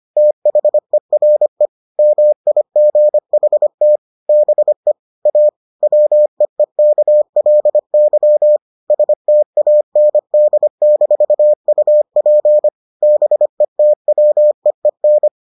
Morse code, heard as a keyed tone, about 25 wpm, 600 Hz. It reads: THERE MIGHT BE A WEEKLY STAND-UP BETWEEN